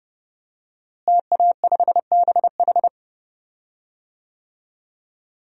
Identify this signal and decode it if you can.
Morse code, heard as a keyed tone, about 30 wpm, 700 Hz. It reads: TA5BH
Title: DXCC-100 Call Signs - 5 Characters - Repeated 2x - 30wpm